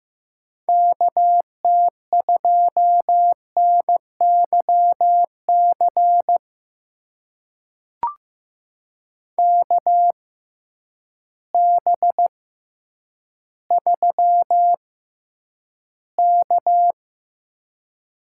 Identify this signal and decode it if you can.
Morse code, heard as a keyed tone, about 15 wpm, spaced as for 9 wpm, 700 Hz. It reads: KT2NYC KB3K
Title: USA Call Signs - 6 Characters - 6x Character Spacing - 15wpm